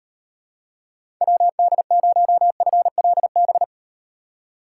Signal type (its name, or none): Morse code